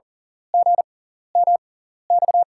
Morse code, heard as a keyed tone, about 37 wpm, spaced as for 13 wpm, 700 Hz. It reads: GMX
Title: USA Call Signs - 6x Character Spacing - 40wpm